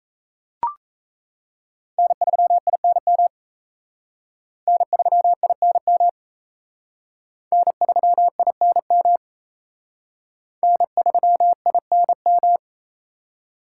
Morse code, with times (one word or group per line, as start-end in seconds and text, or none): 2.0-3.3 N3INM
4.7-6.1 N3INM
7.5-9.2 N3INM
10.6-12.6 N3INM